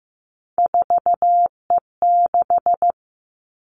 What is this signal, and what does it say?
Morse code, heard as a keyed tone, about 15 wpm, 700 Hz. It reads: 4E6
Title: Three Challenging Letter-Number Combinations 15wpm